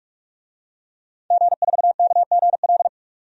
Morse code, heard as a keyed tone, about 46 wpm, 700 Hz. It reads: G4KGL
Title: DXCC-100 Call Signs - 5 Characters - 45wpm